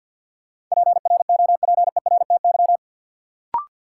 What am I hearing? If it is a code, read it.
Morse code, heard as a keyed tone, about 53 wpm, 700 Hz. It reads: PROPERTY
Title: Sentences from Top 1000 Words - New Word Drill - 50wpm